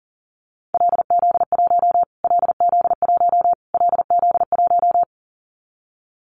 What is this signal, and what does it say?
Morse code, heard as a keyed tone, about 40 wpm, 700 Hz. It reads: LZ1 LZ1 LZ1
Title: Callsign Prefixes for Top 25 DXCC - Repeated 3x - 40wpm